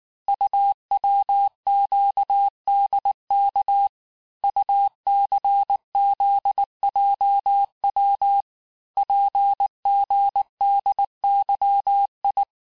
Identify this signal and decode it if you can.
Morse code, heard as a keyed tone, about 19 wpm, 800 Hz. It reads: UWQDK UCZJW PGDYI
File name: cours-du-lundi files / cours-1589-texte-chiffre-vitesse-18mpm